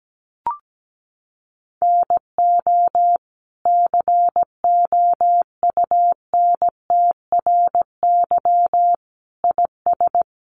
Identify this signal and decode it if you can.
Morse code, heard as a keyed tone, about 17 wpm, 700 Hz. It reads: NO COUNTRY IS
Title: Sentences from Top 900 Words 17wpm